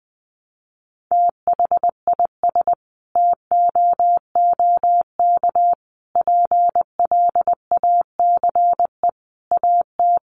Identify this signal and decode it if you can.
Morse code, heard as a keyed tone, about 20 wpm, 700 Hz. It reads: THIS TOOK PLACE AT
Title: Sentences from Top 300 Words 20wpm